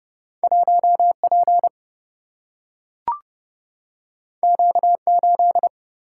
Morse code, heard as a keyed tone, about 30 wpm, 700 Hz. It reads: 1 P Q 8
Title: Single-Letter Single-Number 30wpm